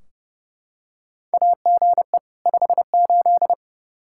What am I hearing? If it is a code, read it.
Morse code, heard as a keyed tone, about 30 wpm, 700 Hz. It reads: AGE 58
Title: Practice QSO - Exchange 3 - More Info - 30wpm